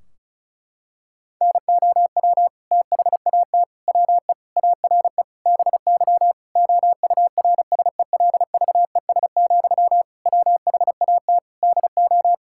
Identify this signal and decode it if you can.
Morse code, heard as a keyed tone, about 34 wpm, 700 Hz. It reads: NOW THAT WE ARE BY OURSELVES, WHAT DO
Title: Sentences from Top 1600 Words 35wpm